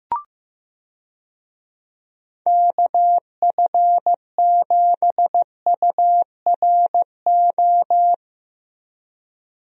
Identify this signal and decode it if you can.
Morse code, heard as a keyed tone, about 15 wpm, 700 Hz. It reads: KF7URO
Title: USA Call Signs - 6 Characters - Repeated 2x - 15wpm